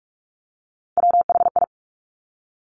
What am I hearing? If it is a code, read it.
Morse code, heard as a keyed tone, about 46 wpm, 690 Hz. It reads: WHI
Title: Top Three Letter Combinations in Words 45wpm